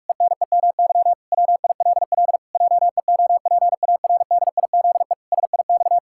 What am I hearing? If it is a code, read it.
Morse code, heard as a keyed tone, about 46 wpm, 700 Hz. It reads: ENEMY WILL JEOPARDIZE SIX
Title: Pangram Challenge v2 - 45wpm